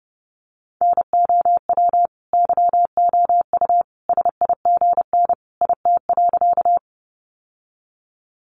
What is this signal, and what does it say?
Morse code, heard as a keyed tone, about 30 wpm, 700 Hz. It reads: NOW YOU SIGN IT.
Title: Sentences from Top 1000 Words 30wpm